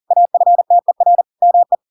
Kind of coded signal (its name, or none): Morse code